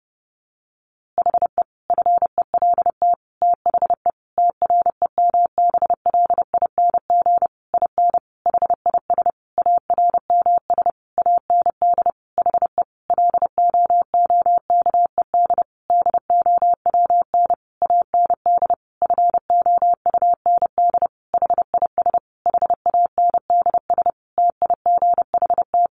Morse code, heard as a keyed tone, about 30 wpm, 700 Hz. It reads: HE FELT THE TREMBLING IN HIS ARMS AND HE LOOKED DOWN AND FOUND HIS HANDS TIGHT
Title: A Sound Of Thunder 30wpm